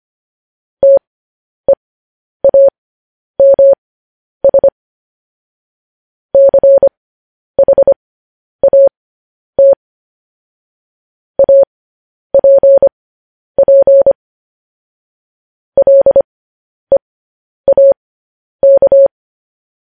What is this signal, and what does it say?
Morse code, heard as a keyed tone, about 25 wpm, spaced as for 10 wpm, 550 Hz. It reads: TEAMS CHAT APP LEAK
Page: news-headlines-in-morse-code-at-10-wpm